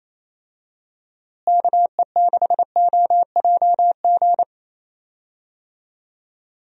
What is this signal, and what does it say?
Morse code, heard as a keyed tone, about 28 wpm, 700 Hz. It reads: KE6OJG
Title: USA Call Signs - 6 Characters - Repeated 3x - 28wpm